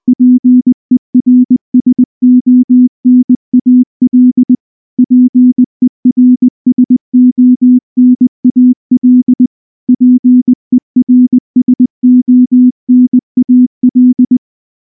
Morse code, heard as a keyed tone, about 19 wpm, 250 Hz. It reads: PERSONAL PERSONAL PERSONAL
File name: morse / morse-training-20WPM-10